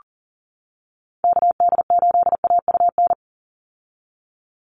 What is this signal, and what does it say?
Morse code, heard as a keyed tone, about 40 wpm, 700 Hz. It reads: KD8AUN
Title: USA Call Signs 40wpm